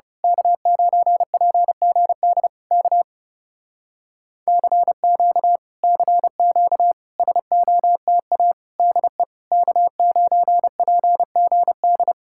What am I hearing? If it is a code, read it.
Morse code, heard as a keyed tone, about 30 wpm, 700 Hz. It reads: K9PGD K CQ CQ SOTA DE K9PGD
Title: Speed-Racing Common QSO Phrases - 30wpm